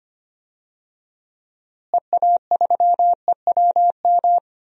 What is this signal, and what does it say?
Morse code, heard as a keyed tone, about 25 wpm, 700 Hz. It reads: EA3EWM
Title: DXCC-100 Call Signs - 6 Characters - Repeated 3x - 25wpm